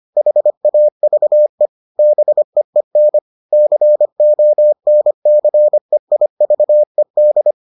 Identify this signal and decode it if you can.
Morse code, heard as a keyed tone, about 25 wpm, 600 Hz. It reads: HAVE BEEN CONCEIVED